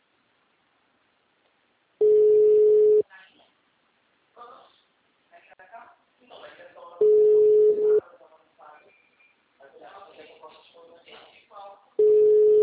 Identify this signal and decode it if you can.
Ringback tone (Europe, 425 Hz), with speech.